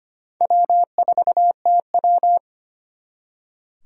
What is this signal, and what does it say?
Morse code, heard as a keyed tone, about 32 wpm, 700 Hz. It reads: W4TW